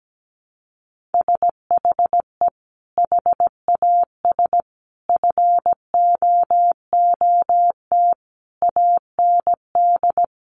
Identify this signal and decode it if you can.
Morse code, heard as a keyed tone, about 17 wpm, 700 Hz. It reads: SHE HAS FOOT AND
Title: Sentences from Top 1500 Words 17wpm